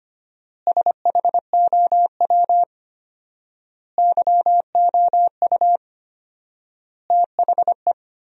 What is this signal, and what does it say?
Morse code, heard as a keyed tone, about 25 wpm, 700 Hz. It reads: SHOW YOU THE
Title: 4x Word-Spacing - Sentences from Top 300 Words - 25wpm